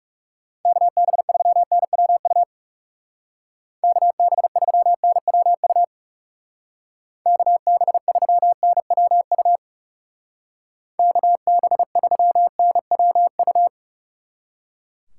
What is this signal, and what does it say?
Morse code, heard as a keyed tone, about 37 wpm, 700 Hz. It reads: KB3NWU KB3NWU KB3NWU KB3NWU
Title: Speed Racing - USA 2x3 Call Signs 30wpm